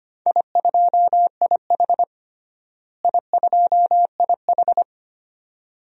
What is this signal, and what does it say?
Morse code, heard as a keyed tone, about 25 wpm, 700 Hz. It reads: I2IH I2IH
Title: DXCC-100 Call Signs - 4 Characters - Repeated 2x - 25wpm